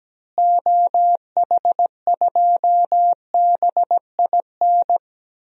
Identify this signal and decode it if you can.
Morse code, heard as a keyed tone, about 17 wpm, 700 Hz. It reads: OH2BIN
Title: DXCC-100 Call Signs - 6 Characters - Repeated 3x - 17wpm